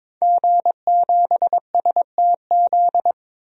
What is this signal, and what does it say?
Morse code, heard as a keyed tone, about 22 wpm, 700 Hz. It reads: G7STZ
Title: DXCC-100 Call Signs - 5 Characters - 22wpm